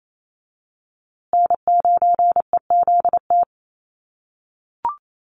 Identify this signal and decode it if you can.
Morse code, heard as a keyed tone, about 28 wpm, 700 Hz. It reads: N9EZT
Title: USA Call Signs Repeated 3x 28wpm